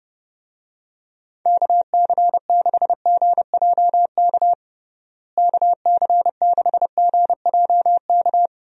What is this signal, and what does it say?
Morse code, heard as a keyed tone, about 30 wpm, 700 Hz. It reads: KC6GJK KC6GJK
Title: USA Call Signs - 6 Characters - Repeated 2x - 30wpm